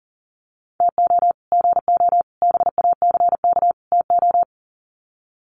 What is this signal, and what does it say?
Morse code, heard as a keyed tone, about 40 wpm, 700 Hz. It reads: TO GO BACK TO
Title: Sets of 4 Words 40wpm